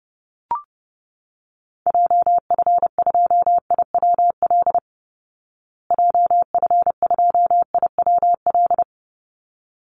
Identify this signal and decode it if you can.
Morse code, heard as a keyed tone, about 30 wpm, 700 Hz. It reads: JF2IWL JF2IWL
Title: Call Signs Repeated 3x - 30wpm